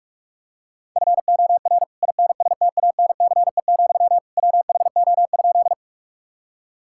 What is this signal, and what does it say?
Morse code, heard as a keyed tone, about 46 wpm, 700 Hz. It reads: FOR INSTANCE, WHO?
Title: Sentences from Top 1000 Words - Review Part 5 - 45wpm